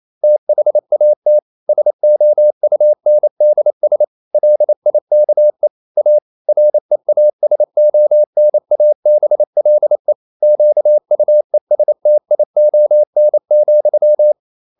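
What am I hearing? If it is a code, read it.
Morse code, heard as a keyed tone, about 28 wpm, 600 Hz. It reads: THAT SOUNDS LIKE A REASONABLE QUESTION,
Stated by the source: Morse code practice at 28 wpm, text of Ken Follett's novel Never in English, from CW med Gnister.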